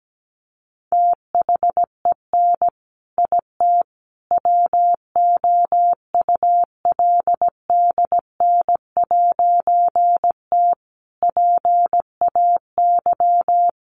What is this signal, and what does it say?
Morse code, heard as a keyed tone, about 17 wpm, 700 Hz. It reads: THEN IT WOULDN'T PAY
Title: A Sound Of Thunder 17wpm